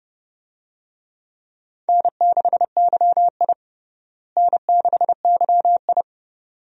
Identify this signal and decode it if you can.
Morse code, heard as a keyed tone, about 32 wpm, 700 Hz. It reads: N6YI N6YI
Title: USA Call Signs - 4 Characters - Repeated 2x - 30wpm